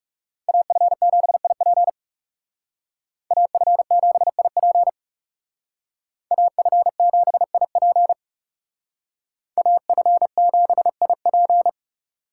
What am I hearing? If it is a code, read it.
Morse code, heard as a keyed tone, about 39 wpm, 700 Hz. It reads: AF7IP AF7IP AF7IP AF7IP